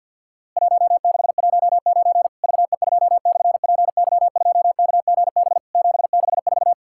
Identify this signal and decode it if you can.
Morse code, heard as a keyed tone, about 50 wpm, 700 Hz. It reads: 1619 VE2/PY2KGB 764
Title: Worked All Europe DX Contest - QTC Exchange with Repeats - 50wpm